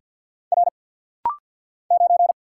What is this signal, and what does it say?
Morse code, heard as a keyed tone, about 48 wpm, 690 Hz. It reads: R 9
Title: Single Letter-Number - Rapid-Fire 50wpm